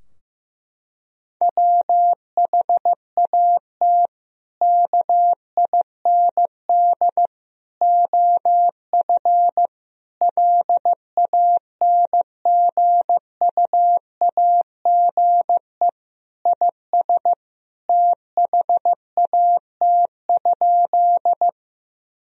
Morse code, heard as a keyed tone, about 15 wpm, 700 Hz. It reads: WHAT KIND OF LANGUAGE IS THAT?
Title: Sentences from Top 1000 Words - Review Part 3 - 15wpm